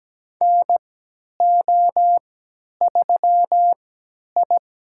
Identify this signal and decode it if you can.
Morse code, heard as a keyed tone, about 17 wpm, spaced as for 10 wpm, 700 Hz. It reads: NO3I
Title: USA Call Signs - 3x Character Spacing - 17wpm